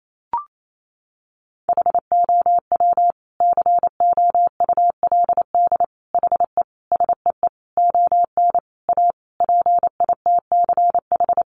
Morse code, heard as a keyed tone, about 28 wpm, 700 Hz. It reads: HOW COULD HE SEE ON A PITCH
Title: Sentences from Top 1300 Words 28wpm